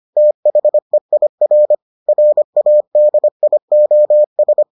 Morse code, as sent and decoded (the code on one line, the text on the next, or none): - .... .. .-. / .-. .- -.. .. --- ...
THEIR RADIOS